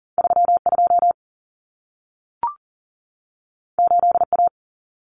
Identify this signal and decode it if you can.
Morse code, heard as a keyed tone, about 40 wpm, 700 Hz. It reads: 32 8A
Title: Common Cut Numbers - 2-Digit 40wpm